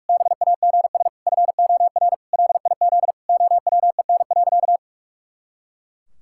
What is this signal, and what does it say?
Morse code, heard as a keyed tone, about 46 wpm, 700 Hz. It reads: BAGS FOR LIZ OWEN.
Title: Pangram Challenge v2 - 45wpm